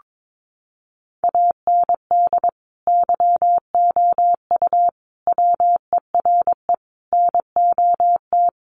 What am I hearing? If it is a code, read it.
Morse code, heard as a keyed tone, about 22 wpm, 700 Hz. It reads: AND YOU WERE NOT
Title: Sentences from Top 1400 Words 22wpm